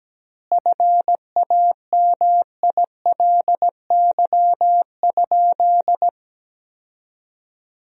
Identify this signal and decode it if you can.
Morse code, heard as a keyed tone, about 17 wpm, 700 Hz. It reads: FAMILY?
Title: 4x Word-Spacing - Sentences from Top 400 Words - 17wpm